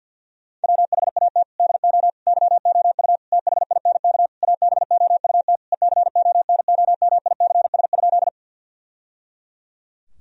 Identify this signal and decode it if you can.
Morse code, heard as a keyed tone, about 50 wpm, 700 Hz. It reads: WHAT DO YOU THINK ABOUT ECONOMICS?
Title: Sentences from Top 1000 Words - Review Part 3 - 50wpm